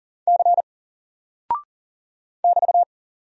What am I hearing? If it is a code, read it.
Morse code, heard as a keyed tone, about 40 wpm, 700 Hz. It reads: C <BT>
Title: Single Character (Letter:Number:Punctuation:Prosign) Alt 40wpm